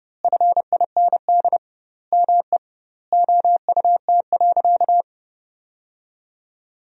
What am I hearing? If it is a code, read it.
Morse code, heard as a keyed tone, about 30 wpm, 700 Hz. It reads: FIND ME OUT.
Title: 2x Word-Spacing - Sentences from Top 500 Words - 30wpm